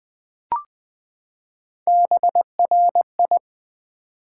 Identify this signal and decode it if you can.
Morse code, heard as a keyed tone, about 20 wpm, 700 Hz. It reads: BRI
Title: Build A Word - 3 Letters At A Time - 9-Letter Words - 20wpm